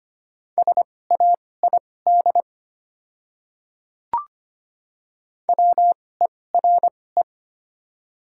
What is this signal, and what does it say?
Morse code, heard as a keyed tone, about 25 wpm, spaced as for 18 wpm, 700 Hz. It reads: SAID WERE
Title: 2x Character Spacing - Top 100 Words - 25wpm